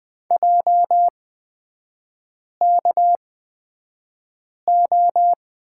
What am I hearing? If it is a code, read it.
Morse code, heard as a keyed tone, about 20 wpm, spaced as for 5 wpm, 690 Hz. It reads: JKO
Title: 229 Lesson 37e - Letter x - Three Characters 20wpm